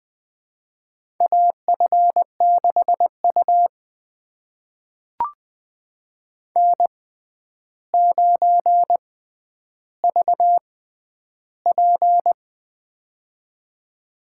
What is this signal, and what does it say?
Morse code, heard as a keyed tone, about 20 wpm, spaced as for 10 wpm, 700 Hz. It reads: AF6U N9VP
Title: USA Call Signs - 4 Characters - 6x Character Spacing - 20wpm